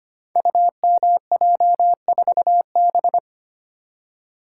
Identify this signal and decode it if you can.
Morse code, heard as a keyed tone, about 25 wpm, 700 Hz. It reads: UMJ4B
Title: Code Groups - 5 Letter:Numbers - 25wpm